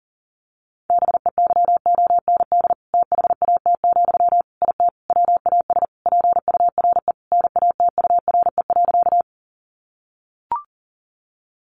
Morse code, heard as a keyed tone, about 40 wpm, 700 Hz. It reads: BEYOND THAT, IT WAS PURE NATURE.
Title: Sentences from Top 1500 Words 40wpm